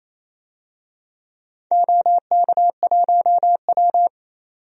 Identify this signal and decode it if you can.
Morse code, heard as a keyed tone, about 28 wpm, 700 Hz. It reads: OK1W
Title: DXCC-100 Call Signs - 4 Characters - 28wpm